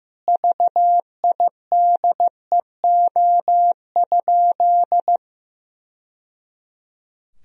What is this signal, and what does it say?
Morse code, heard as a keyed tone, about 16 wpm, 700 Hz. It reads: VIDEO?
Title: Sentences from Top 1000 Words - Review Part 4 - 15wpm